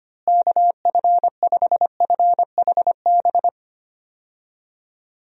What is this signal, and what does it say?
Morse code, heard as a keyed tone, about 25 wpm, 700 Hz. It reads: KF5FHB
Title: USA Call Signs - 6 Characters - Repeated 3x - 25wpm